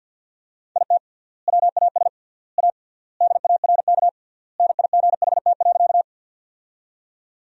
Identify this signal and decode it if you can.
Morse code, heard as a keyed tone, about 53 wpm, 700 Hz. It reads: IT WAS A DARK NIGHT.
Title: 3x Word-Spacing - Sentences from Top 500 Words - 50wpm